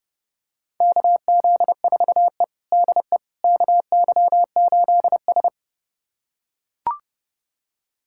Morse code, heard as a keyed tone, about 30 wpm, 700 Hz. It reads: KZ4E DE KY8S
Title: USA Callsign DE Callsign - 4 Characters - 30wpm